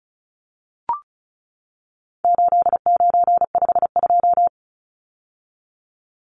Morse code, heard as a keyed tone, about 34 wpm, 700 Hz. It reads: 8952
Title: Numbers - 4-Digit 35wpm